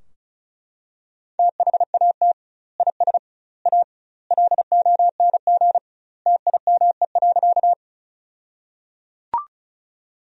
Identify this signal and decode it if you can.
Morse code, heard as a keyed tone, about 36 wpm, 700 Hz. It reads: THAT IS A LONG TIME.
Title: 2x Word-Spacing - Sentences from Top 100 Words - 35wpm